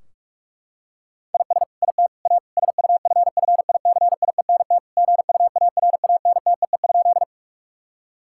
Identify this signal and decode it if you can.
Morse code, heard as a keyed tone, about 41 wpm, 700 Hz. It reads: IS IT A SUFFICIENT GUARANTEE?
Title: Sentences from Top 1600 Words 45wpm